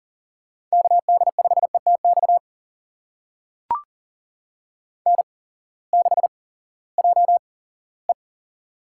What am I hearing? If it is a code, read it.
Morse code, heard as a keyed tone, about 40 wpm, spaced as for 20 wpm, 700 Hz. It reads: KD5ETX N6JE
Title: USA Call Signs - 8x Character Spacing - 40wpm